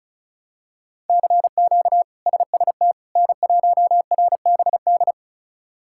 Morse code, heard as a keyed tone, about 36 wpm, 700 Hz. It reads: CQ SST N1RBD
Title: K1USN SST QSO Practice - 35wpm